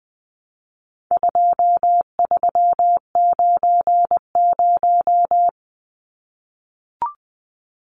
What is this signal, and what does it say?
Morse code, heard as a keyed tone, about 20 wpm, 700 Hz. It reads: 2390
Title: Numbers - 4-Digit 20wpm